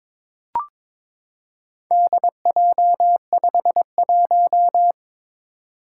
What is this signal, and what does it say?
Morse code, heard as a keyed tone, about 22 wpm, 700 Hz. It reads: DJ51
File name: practice / 22wpm / Four Letter-Number Combinations - 22wpm